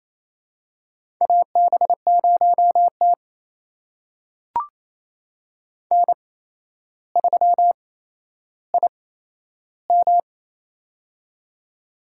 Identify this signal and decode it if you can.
Morse code, heard as a keyed tone, about 28 wpm, spaced as for 12 wpm, 700 Hz. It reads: AB0T N3IM
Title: USA Call Signs - 4 Characters - 8x Character Spacing - 28wpm